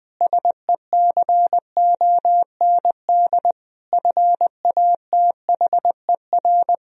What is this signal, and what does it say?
Morse code, heard as a keyed tone, about 20 wpm, 700 Hz. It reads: SECOND FATHER
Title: Sentences from Top 1000 Words - Review Part 1 - 20wpm